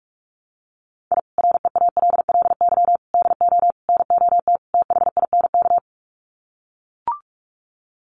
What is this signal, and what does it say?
Morse code, heard as a keyed tone, about 44 wpm, 700 Hz. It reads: I REALLY DO NOT THINK
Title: Sets of 5 Words Encore2 45wpm